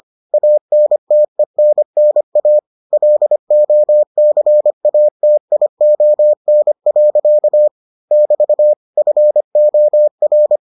Morse code, heard as a keyed tone, about 25 wpm, 600 Hz. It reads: ANTENNA LOCATION. <BT> FOR